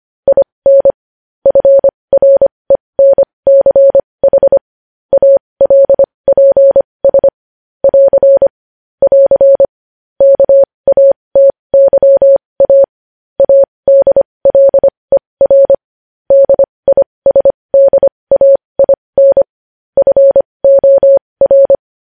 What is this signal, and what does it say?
Morse code, heard as a keyed tone, about 25 wpm, spaced as for 20 wpm, 550 Hz. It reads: IN FRENCH ALPS <AR> <AR> KATYA ADLER DISDAIN FOR
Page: news-headlines-in-morse-code-at-20-wpm